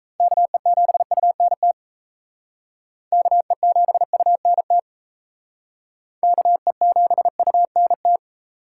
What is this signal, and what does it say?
Morse code, heard as a keyed tone, about 36 wpm, 700 Hz. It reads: KE7UNT KE7UNT KE7UNT